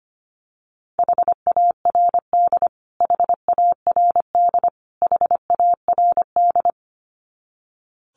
Morse code, heard as a keyed tone, about 25 wpm, 700 Hz. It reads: HARD HARD HARD
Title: Top 500 Words Repeated 25wpm